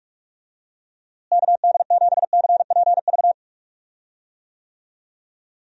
Morse code, heard as a keyed tone, about 41 wpm, 700 Hz. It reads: KD7CPV
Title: USA Call Signs - 6 Characters - Repeated 2x - 45wpm